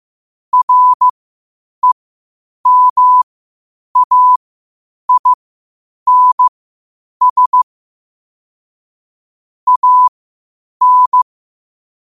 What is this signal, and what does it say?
Morse code, heard as a keyed tone, about 15 wpm, spaced as for 8 wpm, 1000 Hz. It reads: REMAINS AN
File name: qotdfiles / 8wpm